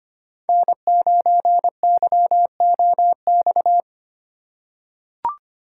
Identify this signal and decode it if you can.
Morse code, heard as a keyed tone, about 25 wpm, 700 Hz. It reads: N9YOX
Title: USA Call Signs - 5 Characters - 4x Character Spacing - 25wpm